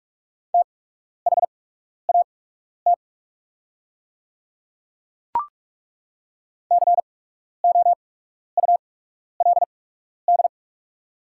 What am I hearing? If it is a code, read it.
Morse code, heard as a keyed tone, about 42 wpm, spaced as for 12 wpm, 690 Hz. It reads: THAT COULD
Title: 8x Character Spacing - Top 200 Words - 45wpm